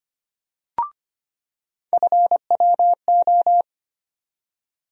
Morse code, heard as a keyed tone, about 25 wpm, 700 Hz. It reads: FWO
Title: USA Call Sign Postfixes - 3 Letters 25wpm